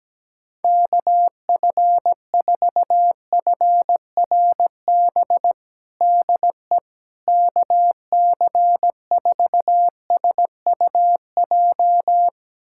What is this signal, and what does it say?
Morse code, heard as a keyed tone, about 17 wpm, 700 Hz. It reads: KF4FRB DE KC4SUJ
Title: USA Callsign DE Callsign - 6 Characters - 17wpm